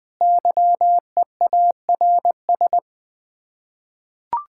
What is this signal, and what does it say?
Morse code, heard as a keyed tone, about 20 wpm, 700 Hz. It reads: YEARS
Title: Build A Phrase - 5-Word Phrases - 20wpm